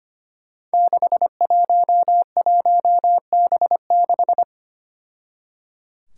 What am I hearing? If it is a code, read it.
Morse code, heard as a keyed tone, about 25 wpm, 700 Hz. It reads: 611B6
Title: Five Challenging Letter-Number Combinations 25wpm